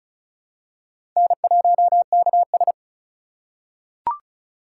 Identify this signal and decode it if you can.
Morse code, heard as a keyed tone, about 36 wpm, 700 Hz. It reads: N1KS
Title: USA Call Signs - 4 Characters - 2x Character Spacing - 35wpm